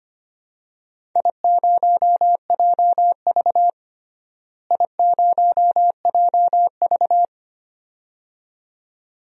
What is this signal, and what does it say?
Morse code, heard as a keyed tone, about 25 wpm, 700 Hz. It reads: I0JV I0JV
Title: DXCC-100 Call Signs - 4 Characters - Repeated 2x - 25wpm